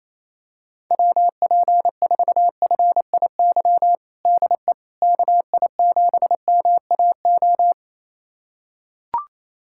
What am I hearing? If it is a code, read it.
Morse code, heard as a keyed tone, about 28 wpm, 700 Hz. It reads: WP4FIY DE KI7MAO
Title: USA Callsign DE Callsign 28wpm